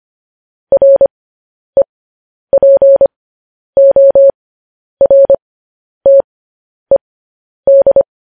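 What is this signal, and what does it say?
Morse code, heard as a keyed tone, about 25 wpm, spaced as for 10 wpm, 550 Hz. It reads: REPORTED
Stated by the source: Morse code transmission of news headlines.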